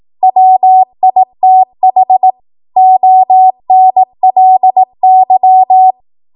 Morse code, heard as a keyed tone, about 18 wpm, 750 Hz. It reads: WITH ONLY
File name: Morse / 18WPM